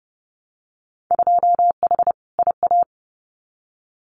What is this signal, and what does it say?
Morse code, heard as a keyed tone, about 30 wpm, 700 Hz. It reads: 2H IA
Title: Winter Field Day Exchanges Encore 30wpm